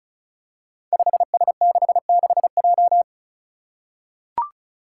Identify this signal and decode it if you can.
Morse code, heard as a keyed tone, about 36 wpm, 700 Hz. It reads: 5S66J
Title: Five Challenging Letter-Number Combinations 35wpm